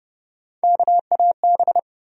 Morse code, heard as a keyed tone, about 29 wpm, 700 Hz. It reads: KAB
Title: USA Call Sign Postfixes - 3 Letters 30wpm